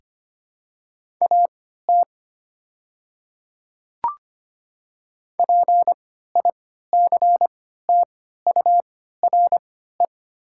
Morse code, heard as a keyed tone, about 25 wpm, spaced as for 14 wpm, 700 Hz. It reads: AT PICTURE
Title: 3x Character Spacing - Top 200 Words - 25wpm